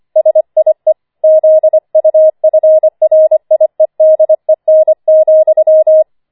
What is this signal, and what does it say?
Morse code, heard as a keyed tone, about 25 wpm, 630 Hz. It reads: SIE ZUFRIEDEN,
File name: 25wpm